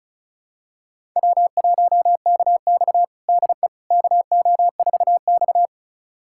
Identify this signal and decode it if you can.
Morse code, heard as a keyed tone, about 36 wpm, 700 Hz. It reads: W1KX DE KO4X
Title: USA Callsign DE Callsign - 4 Characters - 35wpm